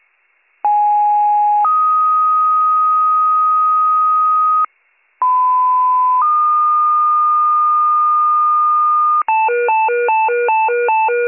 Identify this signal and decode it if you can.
Two-tone page, heard as a one-second tone, then a three-second tone, about 830 then 1250 Hz, about 1010 then 1220 Hz. It is heard from 0.6 to 9.2 s.